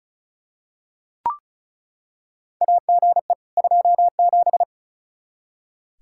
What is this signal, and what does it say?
Morse code, heard as a keyed tone, about 36 wpm, 700 Hz. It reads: AGE 27
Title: K1USN SST QSO Practice 35wpm